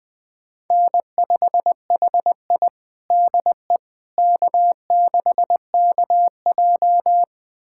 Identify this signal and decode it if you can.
Morse code, heard as a keyed tone, about 21 wpm, 700 Hz. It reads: N5HI DE K6KJ
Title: USA Callsign DE Callsign - 4 Characters - 20wpm